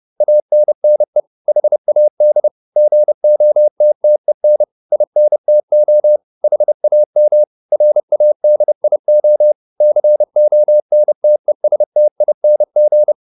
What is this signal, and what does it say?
Morse code, heard as a keyed tone, about 30 wpm, 600 Hz. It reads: ANNE HAD GOTTEN INTO HAM RADIO CONTESTING